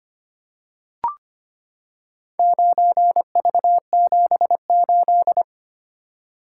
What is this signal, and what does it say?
Morse code, heard as a keyed tone, about 25 wpm, 700 Hz. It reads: 9V78
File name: practice / 25wpm / Four Letter-Number Combinations - 25wpm